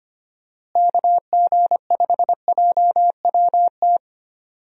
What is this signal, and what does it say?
Morse code, heard as a keyed tone, about 25 wpm, 700 Hz. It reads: KG5JWT
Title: USA Call Signs - 6 Characters - Repeated 3x - 25wpm